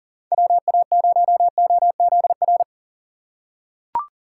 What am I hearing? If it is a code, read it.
Morse code, heard as a keyed tone, about 40 wpm, 700 Hz. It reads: WA0OZR
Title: USA Call Signs - 6 Characters - 8x Character Spacing - 40wpm